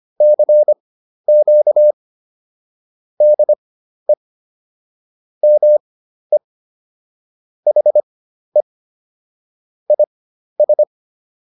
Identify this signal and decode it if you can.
Morse code, heard as a keyed tone, about 25 wpm, spaced as for 12 wpm, 600 Hz. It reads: CQ DE ME HE IS